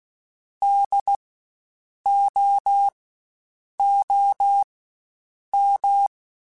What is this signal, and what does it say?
Morse code, heard as a keyed tone, about 16 wpm, spaced as for 7 wpm, 800 Hz. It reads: DOOM
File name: Spaced Alphas / Spaced Alphas mp3s / Ds sp @15